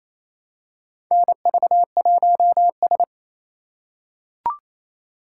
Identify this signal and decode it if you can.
Morse code, heard as a keyed tone, about 28 wpm, 700 Hz. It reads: NV1S
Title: USA Call Signs - 4 Characters - 3x Character Spacing - 28wpm